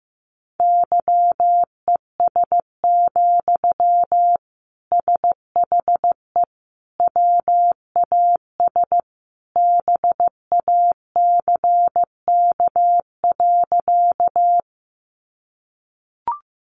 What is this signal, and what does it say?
Morse code, heard as a keyed tone, about 15 wpm, 700 Hz. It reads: YES, SHE WAS BACK.
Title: Sentences from Top 500 Words 15wpm